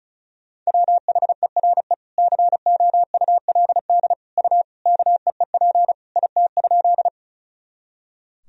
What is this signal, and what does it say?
Morse code, heard as a keyed tone, about 34 wpm, 700 Hz. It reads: WHERE COULD U KEEP IT?
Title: Sentences from Top 300 Words and Abbreviations 35wpm